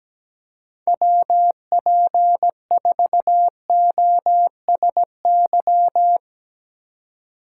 Morse code, heard as a keyed tone, about 17 wpm, 700 Hz. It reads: WP4OSY
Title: Speed Racing - USA 2x3 Call Signs 17wpm